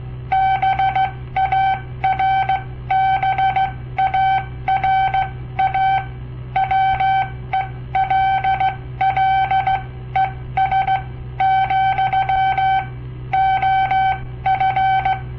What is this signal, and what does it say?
Morse code, heard as a keyed tone, about 16 wpm, spaced as for 13 wpm, 770 Hz. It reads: BARBARA WELLES, OF
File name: papertapes / local / Code Practice-01